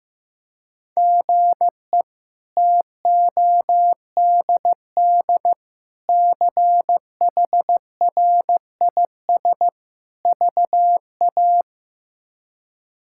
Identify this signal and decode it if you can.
Morse code, heard as a keyed tone, about 15 wpm, 700 Hz. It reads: GE TODD CHRIS VA
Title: K1USN SST QSO Practice - 15wpm